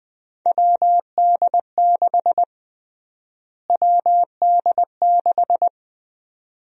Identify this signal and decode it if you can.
Morse code, heard as a keyed tone, about 20 wpm, 700 Hz. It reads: WD6 WD6
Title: USA Call Sign Prefixes - Repeated 2x - 20wpm